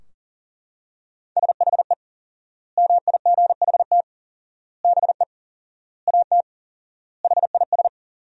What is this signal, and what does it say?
Morse code, heard as a keyed tone, about 40 wpm, 700 Hz. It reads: SHE MIGHT BE AT HIS
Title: 4x Word-Spacing - Sentences from Top 300 Words - 40wpm